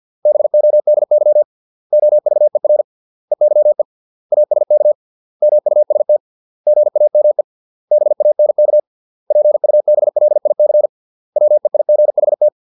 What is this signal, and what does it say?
Morse code, heard as a keyed tone, about 50 wpm, 600 Hz. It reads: BODY OVER EYE ASK MUST GAME BANK PUBLIC WEIGHT